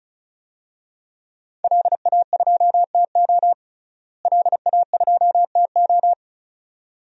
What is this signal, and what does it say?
Morse code, heard as a keyed tone, about 36 wpm, 700 Hz. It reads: LA2TO LA2TO
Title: DXCC-100 Call Signs - 5 Characters - Repeated 2x - 35wpm